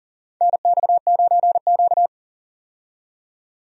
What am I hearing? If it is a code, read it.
Morse code, heard as a keyed tone, about 40 wpm, 700 Hz. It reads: NX9Q
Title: USA Call Signs - 4 Characters - 40wpm